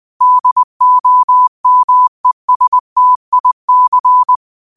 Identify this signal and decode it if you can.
Morse code, heard as a keyed tone, about 20 wpm, 1000 Hz. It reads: DOMESTIC